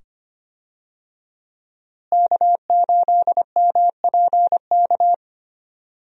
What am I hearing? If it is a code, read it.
Morse code, heard as a keyed tone, about 25 wpm, 700 Hz. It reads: K8MPK
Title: USA Call Signs - 5 Characters - Repeated 2x - 25wpm